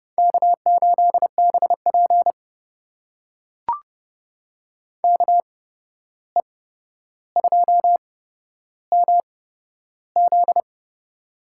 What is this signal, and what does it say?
Morse code, heard as a keyed tone, about 30 wpm, spaced as for 12 wpm, 700 Hz. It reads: K8BP KE2MZ
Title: USA Call Signs - 8x Character Spacing - 30wpm